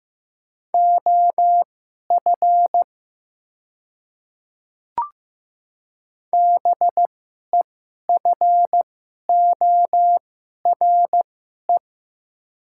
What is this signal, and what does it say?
Morse code, heard as a keyed tone, about 15 wpm, spaced as for 11 wpm, 700 Hz. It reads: OF BEFORE
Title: 2x Character Spacing - Top 200 Words - 15wpm